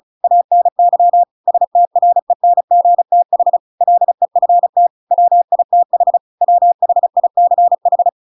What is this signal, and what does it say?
Morse code, heard as a keyed tone, about 34 wpm, 700 Hz. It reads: ANY STRENGTH LEFT WITH WHICH